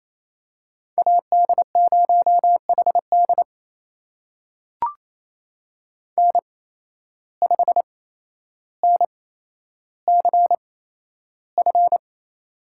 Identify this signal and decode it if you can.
Morse code, heard as a keyed tone, about 28 wpm, spaced as for 12 wpm, 700 Hz. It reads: AD0HD N5NCF
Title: USA Call Signs - 8x Character Spacing - 28wpm